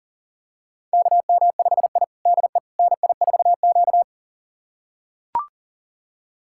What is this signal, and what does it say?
Morse code, heard as a keyed tone, about 40 wpm, 700 Hz. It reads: KM5I DE NI4Q
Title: USA Callsign DE Callsign - 4 Characters - 40wpm